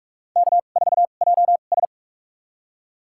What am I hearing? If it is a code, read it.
Morse code, heard as a keyed tone, about 45 wpm, spaced as for 32 wpm, 700 Hz. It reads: K4JS
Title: USA Call Signs - 4 Characters - 2x Character Spacing - 45wpm